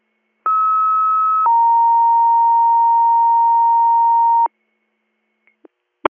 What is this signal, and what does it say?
Two-tone page, heard as a one-second tone, then a three-second tone, about 1290 then 930 Hz. It is heard from 0.4 to 4.5 s.